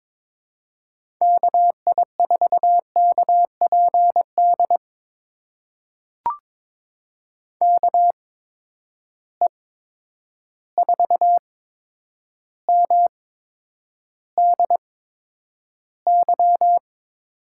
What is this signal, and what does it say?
Morse code, heard as a keyed tone, about 22 wpm, spaced as for 9 wpm, 700 Hz. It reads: KI4KPD KE4MDY
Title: USA Call Signs - 8x Character Spacing - 22wpm